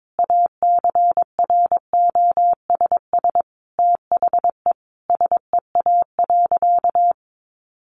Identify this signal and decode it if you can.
Morse code, heard as a keyed tone, about 22 wpm, 700 Hz. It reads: ACROSS THE SEA.
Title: Sentences from Top 1300 Words 22wpm